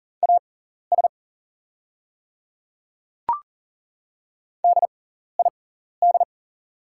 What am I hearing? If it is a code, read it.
Morse code, heard as a keyed tone, about 40 wpm, spaced as for 14 wpm, 690 Hz. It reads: AS DID
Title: 6x Character Spacing - Top 100 Words - 40wpm